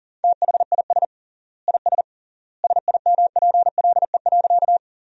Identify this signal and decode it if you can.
Morse code, heard as a keyed tone, about 40 wpm, 700 Hz. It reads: THIS IS SIMPLE.
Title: 3x Word-Spacing - Sentences from Top 400 Words - 40wpm